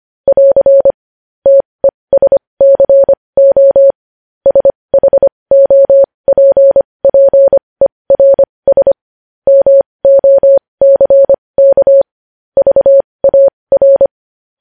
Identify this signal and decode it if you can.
Morse code, heard as a keyed tone, about 25 wpm, spaced as for 20 wpm, 550 Hz. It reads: <AR> TESCO SHOPPERS MOCK VAR
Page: News Headlines in Morse Code at 20 WPM